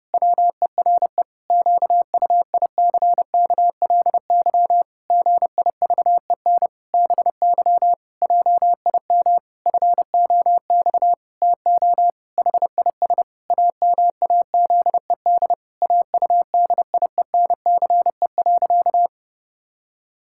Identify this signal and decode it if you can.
Morse code, heard as a keyed tone, about 32 wpm, 700 Hz. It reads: WERE QUICKLY GIVEN BY JIM FOX TO HIS AMAZED AUDIENCE.
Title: Pangram Challenge v2 - 30wpm